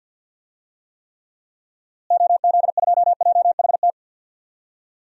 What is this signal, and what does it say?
Morse code, heard as a keyed tone, about 47 wpm, 700 Hz. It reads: OZ2JHT
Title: DXCC-100 Call Signs - 6 Characters - Repeated 2x - 50wpm